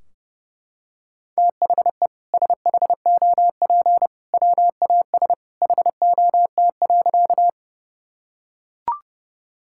Morse code, heard as a keyed tone, about 32 wpm, 700 Hz. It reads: THE SHOP WAS HOT.
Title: Sentences from Top 1600 Words 30wpm